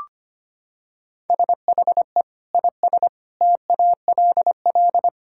Morse code, heard as a keyed tone, about 25 wpm, 700 Hz. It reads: SHE IS TALL
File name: practice / 25wpm / Sentences with 2-Word Phrases - 25wpm